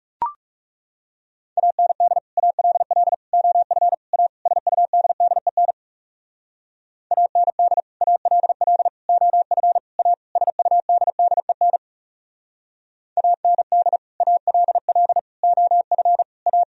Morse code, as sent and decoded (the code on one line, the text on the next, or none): .- -. -.. / .- .-.. .-.. / --- ..-. / .- / ... ..- -.. -.. -. / .- -. -.. / .- .-.. .-.. / --- ..-. / .- / ... ..- -.. -.. -. / .- -. -.. / .- .-.. .-.. / --- ..-. / .-
AND ALL OF A SUDDEN AND ALL OF A SUDDEN AND ALL OF A